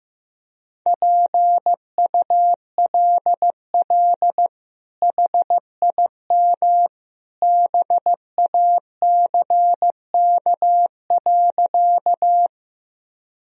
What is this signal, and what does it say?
Morse code, heard as a keyed tone, about 15 wpm, 700 Hz. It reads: PULL HIM BACK.
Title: Sentences from Top 1000 Words - Review Part 2 - 15wpm